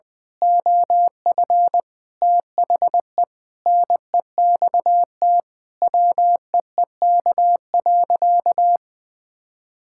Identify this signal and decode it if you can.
Morse code, heard as a keyed tone, about 20 wpm, 700 Hz. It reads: OF THE NEXT WEEK.
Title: Sentences with 6-Word Phrases - Spoken Beginning - 20wpm